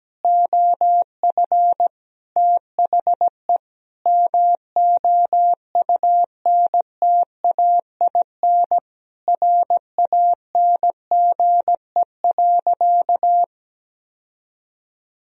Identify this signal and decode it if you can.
Morse code, heard as a keyed tone, about 17 wpm, 700 Hz. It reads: OF THE MOUNTAIN RANGE.
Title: Sentences from Top 1000 Words - Review Part 4 - 17wpm